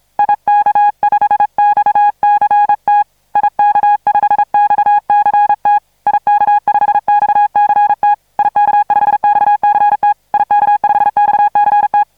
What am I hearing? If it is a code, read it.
Morse code, heard as a keyed tone, about 33 wpm, 830 Hz. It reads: IK5XCT IK5XCT IK5XCT IK5XCT IK5XCT